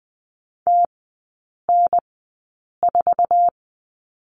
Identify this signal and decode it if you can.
Morse code, heard as a keyed tone, about 20 wpm, spaced as for 8 wpm, 700 Hz. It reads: TN4
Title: 053 Lesson 10e - Letter r - Three Characters 20wpm